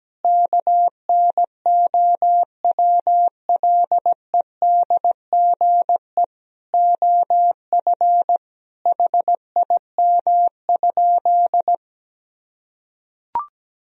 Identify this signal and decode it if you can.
Morse code, heard as a keyed tone, about 17 wpm, 700 Hz. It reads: KNOWLEDGE OF HIM?
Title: Sentences from Top 1000 Words - Review Part 3 - 17wpm